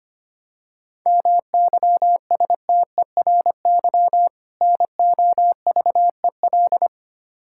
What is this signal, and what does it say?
Morse code, heard as a keyed tone, about 25 wpm, 700 Hz. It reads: MYSTERY NOVEL
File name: practice / 25wpm / Difficult Words with Hints - 25wpm